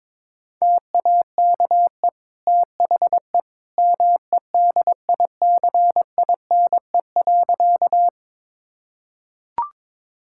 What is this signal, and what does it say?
Morse code, heard as a keyed tone, about 21 wpm, 700 Hz. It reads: TAKE THE MEDICINE.
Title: Sentences from Top 1000 Words - Review Part 6 - 22wpm